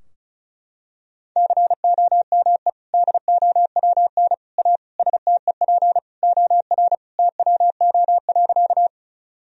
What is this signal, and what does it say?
Morse code, heard as a keyed tone, about 34 wpm, 700 Hz. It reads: COME DOWN A STEP OR TWO.
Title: Sentences from Top 1000 Words - Review Part 2 - 35wpm